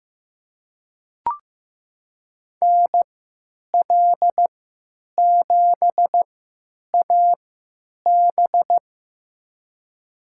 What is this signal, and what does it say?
Morse code, heard as a keyed tone, about 15 wpm, spaced as for 9 wpm, 700 Hz. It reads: NL7AB
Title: USA Call Signs - 5 Characters - 3x Character Spacing - 15wpm